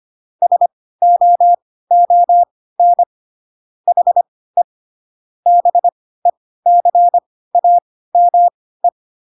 Morse code, heard as a keyed tone, about 25 wpm, spaced as for 16 wpm, 700 Hz. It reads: SOON HE BECAME